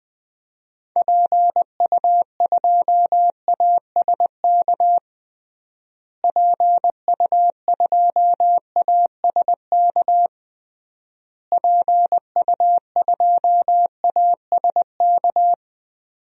Morse code, heard as a keyed tone, about 20 wpm, 700 Hz. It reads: PU2ASK PU2ASK PU2ASK
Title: DXCC-100 Call Signs - 6 Characters - Repeated 3x - 20wpm